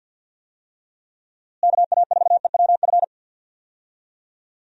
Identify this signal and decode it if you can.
Morse code, heard as a keyed tone, about 46 wpm, 700 Hz. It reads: KA4EWF
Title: USA Call Signs - 6 Characters - Repeated 3x - 50wpm